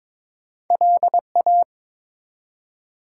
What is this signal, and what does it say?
Morse code, heard as a keyed tone, about 22 wpm, 700 Hz. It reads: LA
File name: ARRL Sections v2 - 22wpm